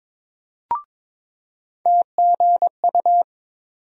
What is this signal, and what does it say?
Morse code, heard as a keyed tone, about 22 wpm, 700 Hz. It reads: TGU
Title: USA Call Sign Postfixes - 3 Letters 22wpm